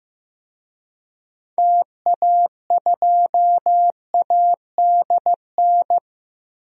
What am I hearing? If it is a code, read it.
Morse code, heard as a keyed tone, about 15 wpm, 700 Hz. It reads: TA2ADN
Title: DXCC-100 Call Signs - 6 Characters - 15wpm